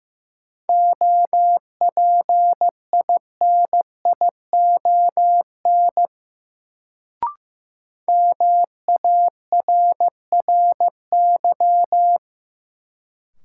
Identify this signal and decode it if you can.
Morse code, heard as a keyed tone, about 15 wpm, 700 Hz. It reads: OPINION MARRY
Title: Sentences from Top 1200 Words - New Word Drill - 15wpm